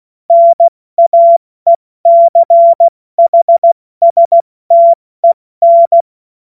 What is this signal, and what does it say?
Morse code, heard as a keyed tone, about 16 wpm, 680 Hz. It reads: NAECHSTEN
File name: Morsekurs / Witze / Witz 21. September 2021, 70 F80